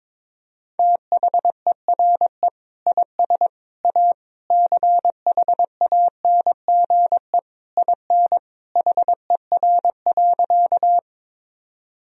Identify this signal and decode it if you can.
Morse code, heard as a keyed tone, about 22 wpm, 700 Hz. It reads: THERE IS A CHANGE IN HER.
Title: Sentences from Top 1000 Words - Review Part 1 - 22wpm